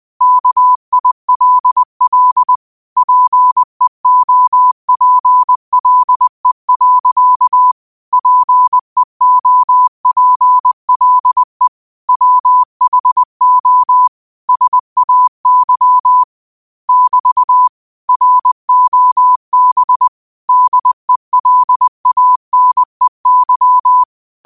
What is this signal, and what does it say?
Morse code, heard as a keyed tone, about 20 wpm, 1000 Hz. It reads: KILL PEOPLE. PEOPLE WHO SAY <BT> ROB DELANEY